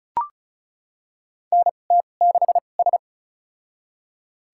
Morse code, heard as a keyed tone, about 36 wpm, spaced as for 25 wpm, 700 Hz. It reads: NT6S
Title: USA Call Signs - 2x Character Spacing - 35wpm